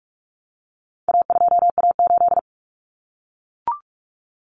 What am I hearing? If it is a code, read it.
Morse code, heard as a keyed tone, about 44 wpm, 700 Hz. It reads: A2A8
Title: Common Cut Numbers - 4-Digit 45wpm